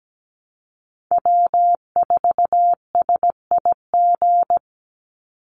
Morse code, heard as a keyed tone, about 17 wpm, 700 Hz. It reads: W4SIG
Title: Call Signs Repeated 3x - 17wpm